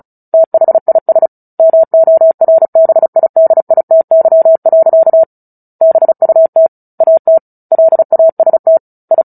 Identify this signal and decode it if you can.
Morse code, heard as a keyed tone, about 34 wpm, 650 Hz. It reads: THIS MORBIDITY. BUT AT LAST I
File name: Book Two / 09-Chapter 9 - 35 WPM